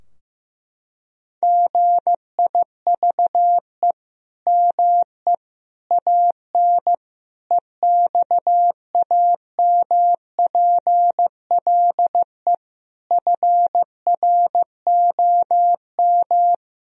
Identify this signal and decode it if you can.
Morse code, heard as a keyed tone, about 15 wpm, 700 Hz. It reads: GIVE ME AN EXAMPLE FROM
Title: Sentences from Top 1000 Words - Review Part 3 - 15wpm